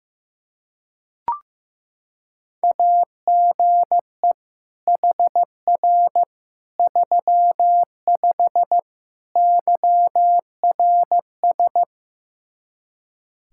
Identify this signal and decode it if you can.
Morse code, heard as a keyed tone, about 15 wpm, 700 Hz. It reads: AGE HR 35 YRS
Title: Practice QSO - Exchange 3 - More Info - 15wpm